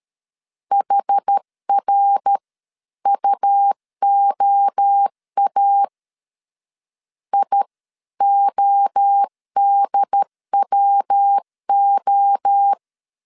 Morse code, heard as a keyed tone, about 13 wpm, spaced as for 11 wpm, 800 Hz. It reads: HRUOA IODWO
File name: CW / DET / 1 3 1500C S A 12wpm 2905